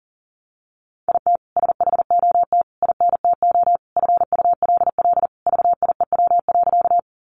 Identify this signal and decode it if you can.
Morse code, heard as a keyed tone, about 40 wpm, 700 Hz. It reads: IT SHOT INTO FULL VIEW.
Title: Sentences from Top 1100 Words 40wpm